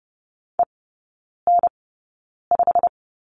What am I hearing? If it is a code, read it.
Morse code, heard as a keyed tone, about 30 wpm, spaced as for 9 wpm, 700 Hz. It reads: EN5
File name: Lesson 17e - Letter u - Three Characters 30wpm